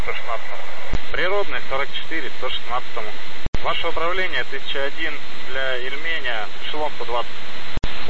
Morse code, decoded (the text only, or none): none